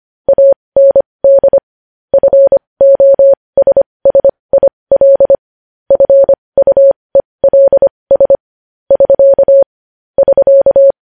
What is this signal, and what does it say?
Morse code, heard as a keyed tone, about 25 wpm, spaced as for 20 wpm, 550 Hz. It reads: AND FOSSIL FUELS <SK> <SK>
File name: podcast / News headlines-1x-25x20